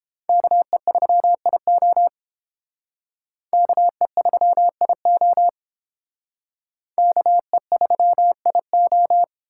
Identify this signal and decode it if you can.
Morse code, heard as a keyed tone, about 28 wpm, 700 Hz. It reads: KE3IO KE3IO KE3IO